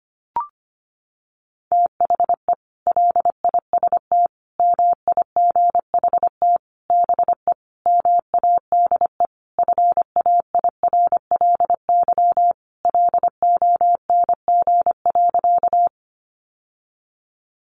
Morse code, heard as a keyed tone, about 25 wpm, 700 Hz. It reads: THE LIST MIGHT BE MADE FAIRLY LONG.
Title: Sentences from Top 1100 Words 25wpm